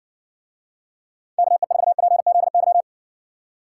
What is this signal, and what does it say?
Morse code, heard as a keyed tone, about 60 wpm, 700 Hz. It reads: KE4PZY